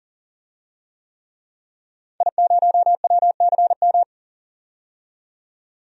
Morse code, heard as a keyed tone, about 40 wpm, 700 Hz. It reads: I0WCM
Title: DXCC-100 Call Signs - 5 Characters - 40wpm